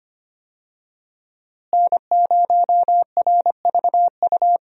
Morse code, heard as a keyed tone, about 25 wpm, 700 Hz. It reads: N0RVU
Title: USA Call Signs - 5 Characters - Repeated 3x - 25wpm